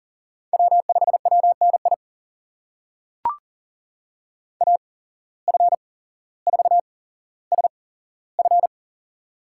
Morse code, heard as a keyed tone, about 40 wpm, spaced as for 17 wpm, 700 Hz. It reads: W5WNI AF4SF
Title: USA Call Signs - 5 Characters - 8x Character Spacing - 40wpm